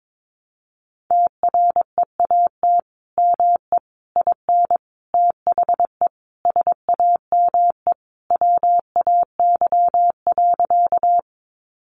Morse code, heard as a keyed tone, about 22 wpm, 700 Hz. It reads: TREAT ME IN THE SAME WAY.
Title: Sentences from Top 1300 Words 22wpm